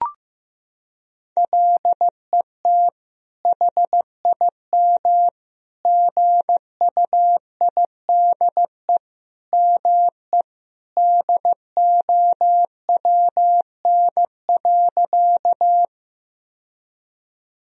Morse code, heard as a keyed tone, about 15 wpm, 700 Hz. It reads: LET HIM GUIDE ME DOWN.
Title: Sentences from Top 1000 Words - Review Part 3 - 15wpm